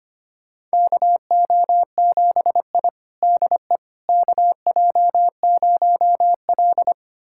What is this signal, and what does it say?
Morse code, heard as a keyed tone, about 25 wpm, 700 Hz. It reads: KO7I DE KJ0L
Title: USA Callsign DE Callsign - 4 Characters - 25wpm